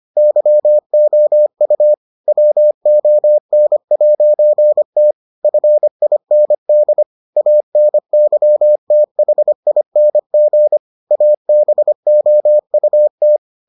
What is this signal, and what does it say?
Morse code, heard as a keyed tone, about 25 wpm, 600 Hz. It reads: YOU WON'T FIND ANYTHING ABOUT